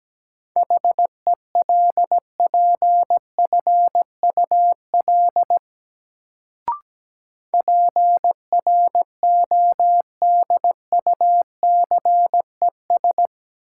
Morse code, heard as a keyed tone, about 17 wpm, 700 Hz. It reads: HELPFUL PRODUCES
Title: Sentences from Top 1000 Words - Review Part 6 - New Word Drill - 17wpm